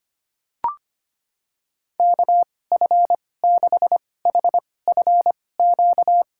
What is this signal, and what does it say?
Morse code, heard as a keyed tone, about 25 wpm, spaced as for 18 wpm, 700 Hz. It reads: KF6HFQ
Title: USA Call Signs - 6 Characters - 2x Character Spacing - 25wpm